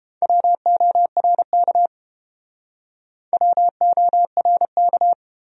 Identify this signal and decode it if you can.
Morse code, heard as a keyed tone, about 29 wpm, 700 Hz. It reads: WORK WORK